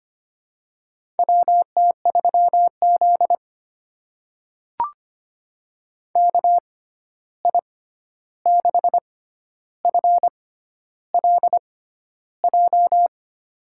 Morse code, heard as a keyed tone, about 25 wpm, spaced as for 11 wpm, 700 Hz. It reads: WT3Z KI6FLJ
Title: USA Call Signs - 6x Character Spacing - 25wpm